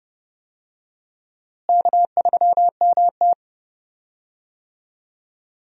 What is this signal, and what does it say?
Morse code, heard as a keyed tone, about 31 wpm, 700 Hz. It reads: K3MT
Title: USA Call Signs - 4 Characters - Repeated 2x - 30wpm